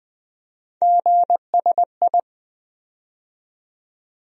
Morse code, heard as a keyed tone, about 20 wpm, 700 Hz. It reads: GSI
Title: USA Call Sign Postfixes - 3 Letters 20wpm